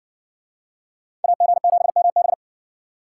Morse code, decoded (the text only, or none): AG7MB